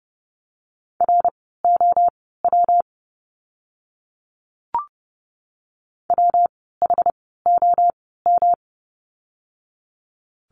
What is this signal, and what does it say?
Morse code, heard as a keyed tone, about 30 wpm, spaced as for 17 wpm, 700 Hz. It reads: ROW WHOM
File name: course / 30:10wpm / 115 Lesson 19f - Letter w - Words 30wpm